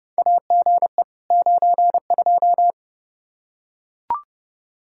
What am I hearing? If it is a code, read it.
Morse code, heard as a keyed tone, about 32 wpm, 700 Hz. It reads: AGE 92
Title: Practice QSO - Exchange 3 - More Info - 30wpm